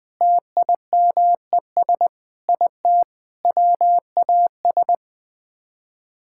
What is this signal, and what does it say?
Morse code, heard as a keyed tone, about 21 wpm, 700 Hz. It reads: TIMES IT WAS
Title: Sets of 8 Words - 20wpm